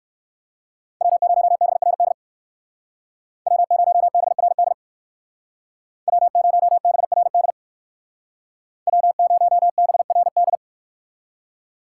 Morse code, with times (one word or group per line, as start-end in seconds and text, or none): 1.0-2.1 W0BRD
3.5-4.7 W0BRD
6.1-7.5 W0BRD
8.9-10.6 W0BRD